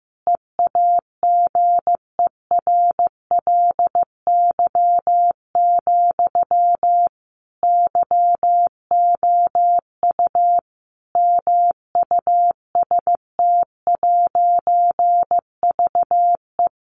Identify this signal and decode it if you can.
Morse code, heard as a keyed tone, about 15 wpm, 700 Hz. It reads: EAGERLY, YOU MUST'VE
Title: A Sound Of Thunder 15wpm